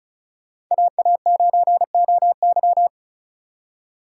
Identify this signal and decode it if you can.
Morse code, heard as a keyed tone, about 45 wpm, 700 Hz. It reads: AA9OY